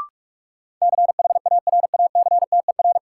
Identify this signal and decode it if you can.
Morse code, heard as a keyed tone, about 46 wpm, 700 Hz. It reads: CHARACTER
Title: Sentences from Top 1200 Words - New Word Drill - 45wpm